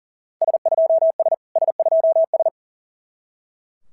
Morse code, heard as a keyed tone, about 42 wpm, 650 Hz. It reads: S2S S2S
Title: SOTA QSO Practice 40wpm